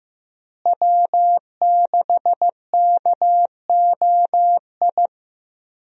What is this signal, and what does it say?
Morse code, heard as a keyed tone, about 15 wpm, 700 Hz. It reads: W6KOI
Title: USA Call Signs - 5 Characters - Repeated 3x - 15wpm